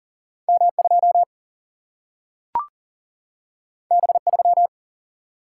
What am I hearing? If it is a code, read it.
Morse code, heard as a keyed tone, about 39 wpm, 700 Hz. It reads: M2 B3
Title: Single-Letter Single-Number 40wpm